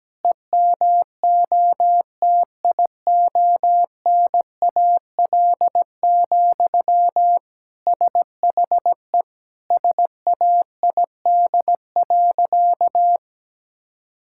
Morse code, heard as a keyed tone, about 17 wpm, 700 Hz. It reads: EMOTIONAL, SHE SAID.
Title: Sentences from Top 1000 Words - Review Part 5 - 17wpm